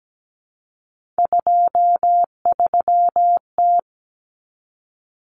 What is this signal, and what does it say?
Morse code, heard as a keyed tone, about 17 wpm, 700 Hz. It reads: 23T
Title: Common Cut Numbers - 3-Digit 17wpm